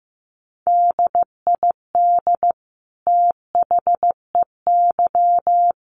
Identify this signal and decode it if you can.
Morse code, heard as a keyed tone, about 14 wpm, 700 Hz. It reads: DID THEY
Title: Sentences from Top 1400 Words 15wpm